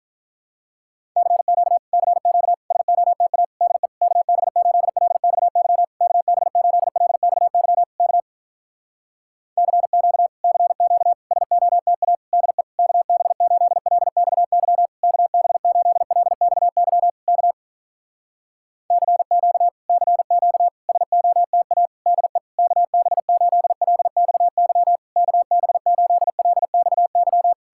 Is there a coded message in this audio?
Yes: Morse code from 1.2 to 27.5 s, heard as a keyed tone, about 46 wpm, 700 Hz. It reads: CQ CQ SOTA DE KB8LXY KB8LXY K CQ CQ SOTA DE KB8LXY KB8LXY K CQ CQ SOTA DE KB8LXY KB8LXY